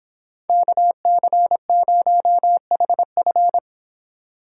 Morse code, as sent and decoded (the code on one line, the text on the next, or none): -.- -.-. ----- .... ..-.
KC0HF